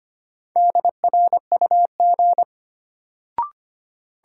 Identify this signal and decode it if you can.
Morse code, heard as a keyed tone, about 25 wpm, 700 Hz. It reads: DRUG
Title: Sets of 5 Words - New Word Drill - 25wpm